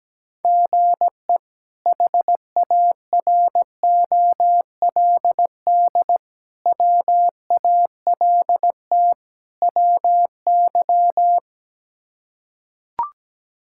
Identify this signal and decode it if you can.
Morse code, heard as a keyed tone, about 17 wpm, 700 Hz. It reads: GE HAROLD WALT WY
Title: K1USN SST QSO Practice - 17wpm